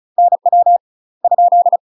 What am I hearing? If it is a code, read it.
Morse code, heard as a keyed tone, about 34 wpm, 700 Hz. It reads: NW ?
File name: UK MASTER QSOs TRAINING / 35WPM / UK MASTER QSOs 0025